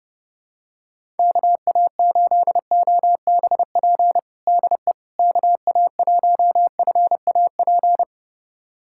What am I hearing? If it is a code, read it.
Morse code, heard as a keyed tone, about 32 wpm, 700 Hz. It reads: KA8OBP DE KA1FAP
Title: USA Callsign DE Callsign - 6 Characters - 30wpm